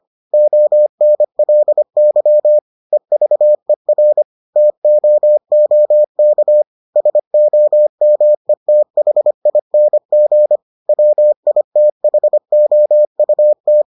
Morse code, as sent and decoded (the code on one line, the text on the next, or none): --- -. .-.. -.-- / . ...- . .-. / - --- --- -.- / ... --- -- . - .... .. -. --. / .-- .. - .... --- ..- -
ONLY EVER TOOK SOMETHING WITHOUT